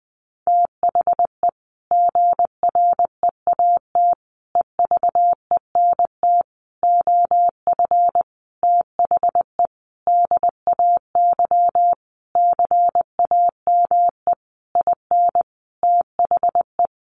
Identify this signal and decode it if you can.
Morse code, heard as a keyed tone, about 20 wpm, 700 Hz. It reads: THE GREAT EVENT OF THE DAY CAME IN THE
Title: Sentences from Top 1500 Words 20wpm